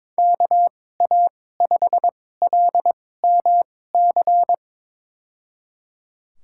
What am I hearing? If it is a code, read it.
Morse code, heard as a keyed tone, about 22 wpm, spaced as for 16 wpm, 700 Hz. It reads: KA5LMC
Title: USA Call Signs - 6 Characters - 2x Character Spacing - 22wpm